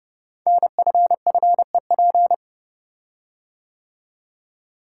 Morse code, heard as a keyed tone, about 30 wpm, 700 Hz. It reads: NFFEP
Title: Code Groups - 5 Letters - 30wpm